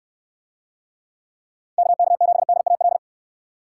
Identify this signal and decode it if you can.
Morse code, heard as a keyed tone, about 63 wpm, 700 Hz. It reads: KK7DAL